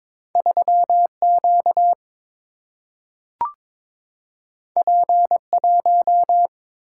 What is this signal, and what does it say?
Morse code, heard as a keyed tone, about 22 wpm, 700 Hz. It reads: 3 Q P 1
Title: Single-Letter Single-Number 22wpm